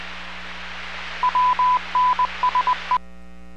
Morse code, heard as a keyed tone, about 20 wpm, 1000 Hz, with noise. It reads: WNSE